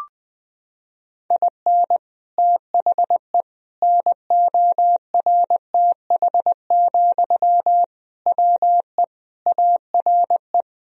Morse code, heard as a keyed tone, about 19 wpm, 700 Hz. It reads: IN THE NORTH, WE ARE
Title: Sentences from Top 1000 Words - Review Part 1 - 20wpm